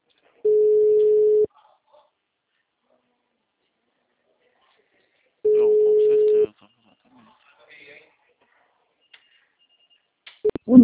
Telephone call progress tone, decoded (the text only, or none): ringback tone